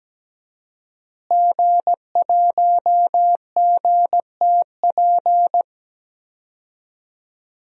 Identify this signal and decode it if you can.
Morse code, heard as a keyed tone, about 17 wpm, 700 Hz. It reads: G1GTP
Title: DXCC-100 Call Signs - 5 Characters - Repeated 3x - 17wpm